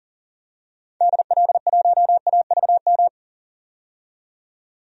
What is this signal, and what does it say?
Morse code, heard as a keyed tone, about 40 wpm, 700 Hz. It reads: DL1AVM
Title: DXCC-100 Call Signs - 6 Characters - 40wpm